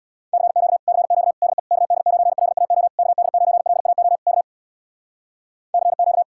Morse code, heard as a keyed tone, about 65 wpm, 700 Hz. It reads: CQ CQ DE KD9LAO KD9LAO K CQ